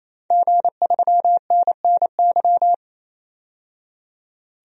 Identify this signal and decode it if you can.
Morse code, heard as a keyed tone, about 27 wpm, 690 Hz. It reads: G3NNY
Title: DXCC-100 Call Signs - 5 Characters - 28wpm